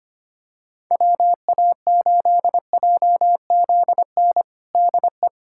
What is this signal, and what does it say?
Morse code, heard as a keyed tone, about 25 wpm, 700 Hz. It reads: WA8JZN DE
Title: USA Callsign DE Callsign - 6 Characters - 25wpm